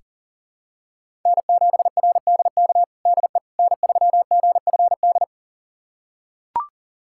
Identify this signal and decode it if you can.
Morse code, heard as a keyed tone, about 37 wpm, 700 Hz. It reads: N7RDK DE N3GFD
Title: USA Callsign DE Callsign - 5 Characters - 40wpm